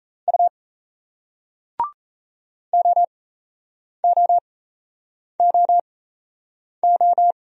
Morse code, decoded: U OOOO

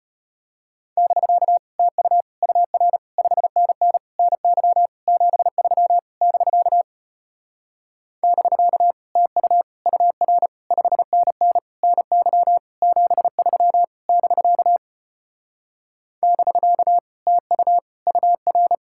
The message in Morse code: <BK> TU UR 5NN NY 73 <BK> <BK> TU UR 5NN NY 73 <BK> <BK> TU UR

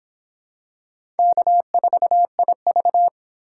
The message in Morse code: K4IV

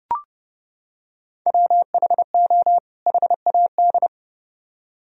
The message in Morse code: WHO HAD

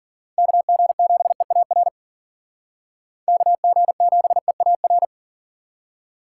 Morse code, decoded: KG7EAR KG7EAR

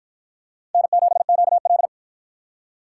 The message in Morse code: N7QL

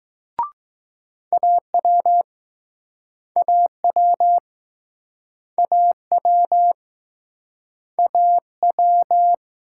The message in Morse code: AW AW AW AW

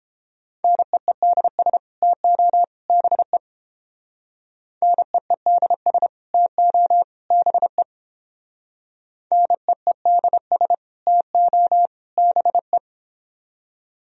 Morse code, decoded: NEEDS TO BE NEEDS TO BE NEEDS TO BE

